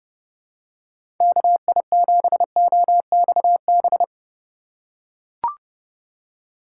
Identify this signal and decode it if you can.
Morse code, heard as a keyed tone, about 30 wpm, 700 Hz. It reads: KI7OXB
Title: USA Call Signs - 6 Characters - 4x Character Spacing - 30wpm